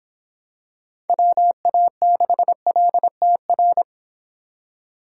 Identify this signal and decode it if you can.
Morse code, heard as a keyed tone, about 26 wpm, 700 Hz. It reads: WA6LTR